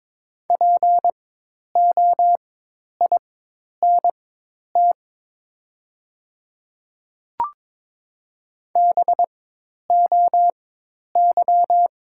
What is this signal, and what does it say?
Morse code, heard as a keyed tone, about 22 wpm, spaced as for 10 wpm, 700 Hz. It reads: POINT BOY
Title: 4x Character Spacing - Top 200 Words - 22wpm